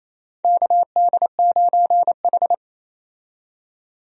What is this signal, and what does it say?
Morse code, heard as a keyed tone, about 28 wpm, 700 Hz. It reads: KD9H
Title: USA Call Signs - 4 Characters - 28wpm